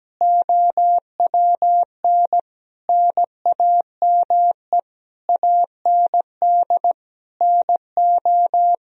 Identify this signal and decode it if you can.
Morse code, heard as a keyed tone, about 17 wpm, 700 Hz. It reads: OWN NAME AND NO
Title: Sentences from Top 1000 Words - Review Part 1 - 17wpm